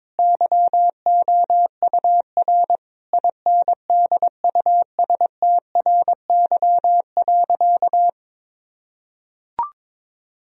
Morse code, heard as a keyed tone, about 22 wpm, 700 Hz. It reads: YOUR INDUSTRY.
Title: Sentences from Top 1000 Words - Review Part 4 - 22wpm